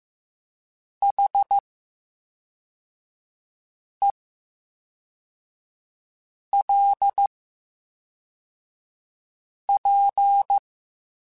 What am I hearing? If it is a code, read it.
Morse code, heard as a keyed tone, about 15 wpm, 800 Hz. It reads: H E L P
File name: Special INGs / Special INGs-mp3s / VerySlowToFaster Ing @10